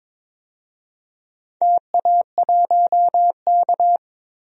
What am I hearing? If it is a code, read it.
Morse code, heard as a keyed tone, about 22 wpm, 700 Hz. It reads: TA1K
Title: DXCC-100 Call Signs - 4 Characters - 22wpm